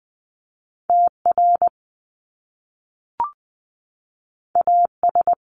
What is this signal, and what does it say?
Morse code, heard as a keyed tone, about 20 wpm, 700 Hz. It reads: TR AS
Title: Top Two Letter Combinations in Words 20wpm